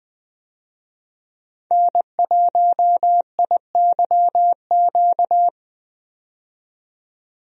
Morse code, heard as a keyed tone, about 20 wpm, 700 Hz. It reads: N1IYQ
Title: USA Call Signs - 5 Characters - 20wpm